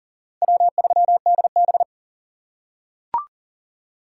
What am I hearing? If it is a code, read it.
Morse code, heard as a keyed tone, about 40 wpm, 700 Hz. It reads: W3DB
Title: USA Call Signs - 4 Characters - 2x Character Spacing - 40wpm